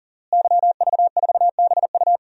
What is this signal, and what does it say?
Morse code, heard as a keyed tone, about 40 wpm, 700 Hz. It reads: YV4BU
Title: DXCC-100 Call Signs - 5 Characters - Repeated 3x - 40wpm